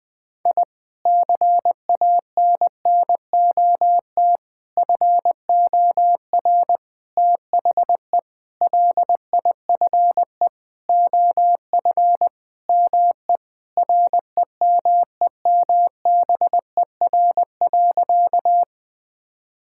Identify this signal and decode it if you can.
Morse code, heard as a keyed tone, about 21 wpm, 700 Hz. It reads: I CANNOT FOR THE LIFE OF ME REMEMBER.
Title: Sentences with 8-Word Phrases - 20wpm